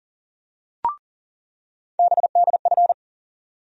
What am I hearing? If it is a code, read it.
Morse code, heard as a keyed tone, about 40 wpm, 700 Hz. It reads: BDF
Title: USA Call Sign Postfixes - 3 Letters 40wpm